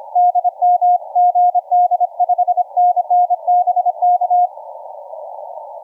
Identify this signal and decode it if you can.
Morse code, heard as a keyed tone, about 27 wpm, 720 Hz, with radio noise. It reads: DM GD 5NN BK
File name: UK R1ZY 0801UTC 20m